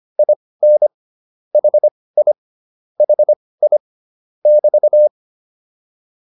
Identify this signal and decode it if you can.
Morse code, heard as a keyed tone, about 25 wpm, spaced as for 18 wpm, 600 Hz. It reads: IN HI HI <BT>
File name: QSO104 18